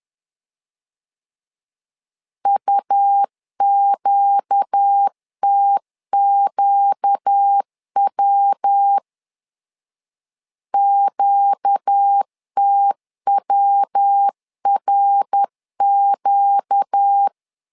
Morse code, heard as a keyed tone, about 10 wpm, 800 Hz. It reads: UQTQW QTWRQ
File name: LEZIONI / 4 Q 10wpm 2210